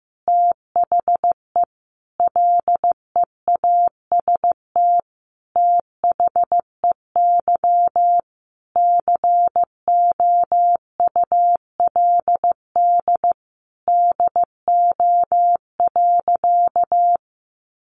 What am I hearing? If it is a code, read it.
Morse code, heard as a keyed tone, about 15 wpm, 700 Hz. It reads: THE LEAST THEY COULD DO.
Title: Sentences from Top 700 Words 15wpm